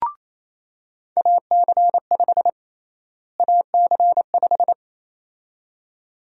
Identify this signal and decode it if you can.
Morse code, heard as a keyed tone, about 28 wpm, 700 Hz. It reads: AC5 AC5
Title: USA Call Sign Prefixes - Repeated 2x - 28wpm